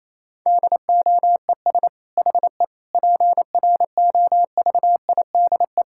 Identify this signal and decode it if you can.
Morse code, heard as a keyed tone, about 28 wpm, 700 Hz. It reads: DOES HE PROVIDE
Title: Sentences from Top 1000 Words - Review Part 6 - 28wpm